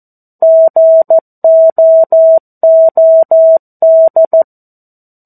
Morse code, heard as a keyed tone, about 14 wpm, 650 Hz. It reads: GOOD
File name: 02-Chapter-02 14 WPM